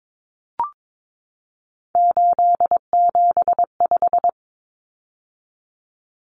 Morse code, heard as a keyed tone, about 22 wpm, 700 Hz. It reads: 875
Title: Numbers - 3-Digit 22wpm